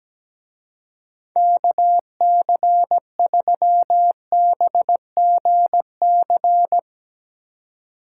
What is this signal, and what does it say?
Morse code, heard as a keyed tone, about 18 wpm, 700 Hz. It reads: KC3BGC
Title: Speed Racing - USA 2x3 Call Signs 17wpm